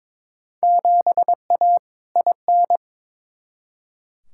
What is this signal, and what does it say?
Morse code, heard as a keyed tone, about 22 wpm, 700 Hz. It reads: 7A IN
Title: Field Day Exchanges Encore - 22wpm